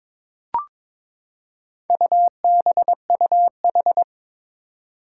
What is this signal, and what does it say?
Morse code, heard as a keyed tone, about 21 wpm, 700 Hz. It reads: UBUH
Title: Four Challenging Letter-Number Combinations 22wpm